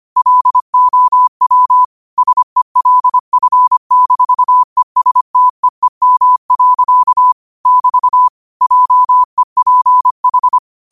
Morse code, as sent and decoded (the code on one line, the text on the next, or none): .-.. --- .-- / ... .-.. ..-. -....- . ... - . -- .-.-.- / -...- / .--- . .--. ....
LOW SELF-ESTEEM. <BT> JEPH